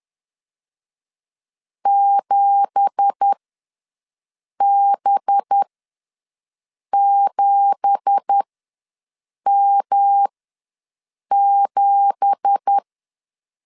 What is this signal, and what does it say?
Morse code, heard as a keyed tone, about 10 wpm, spaced as for 5 wpm, 800 Hz. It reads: 7B7M7
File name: LEZIONI / 9 2867 ws 10wpm 4215